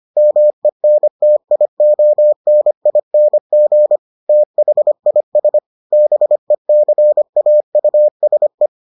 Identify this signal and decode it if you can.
Morse code, heard as a keyed tone, about 25 wpm, 600 Hz. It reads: MENTIONING THIS BECAUSE